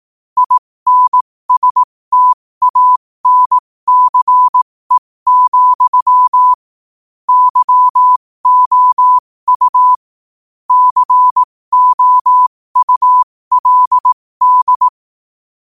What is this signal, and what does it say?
Morse code, heard as a keyed tone, about 18 wpm, spaced as for 15 wpm, 1000 Hz. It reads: INSTANCE, YOU COULD
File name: qotdfiles / 15wpm